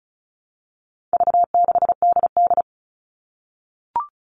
Morse code, heard as a keyed tone, about 34 wpm, 700 Hz. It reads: V6DD
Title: All Cut Numbers - 4-Digit 35wpm